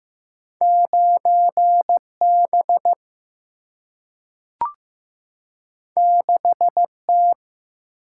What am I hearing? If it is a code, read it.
Morse code, heard as a keyed tone, about 15 wpm, 700 Hz. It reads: 9 B 6 T
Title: Single-Letter Single-Number 15wpm